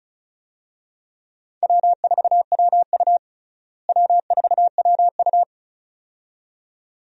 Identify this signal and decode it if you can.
Morse code, heard as a keyed tone, about 34 wpm, 700 Hz. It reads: W4WU W4WU
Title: USA Call Signs - 4 Characters - Repeated 2x - 35wpm